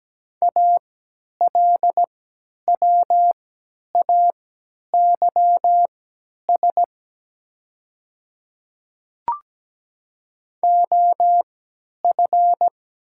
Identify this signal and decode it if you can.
Morse code, heard as a keyed tone, about 17 wpm, spaced as for 10 wpm, 700 Hz. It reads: ALWAYS OF
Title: 3x Character Spacing - Top 300 Words - 17wpm